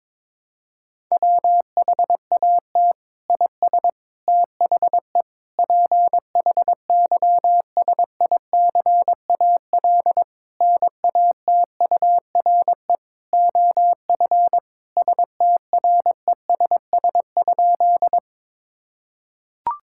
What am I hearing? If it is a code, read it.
Morse code, heard as a keyed tone, about 22 wpm, 700 Hz. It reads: WHAT IS THE PHYSICAL NATURE OF STRESS?
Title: Sentences from Top 1000 Words - Review Part 6 - 22wpm